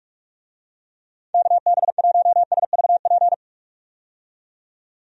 Morse code, heard as a keyed tone, about 45 wpm, 700 Hz. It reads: KB1SVP
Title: USA Call Signs - 6 Characters - 45wpm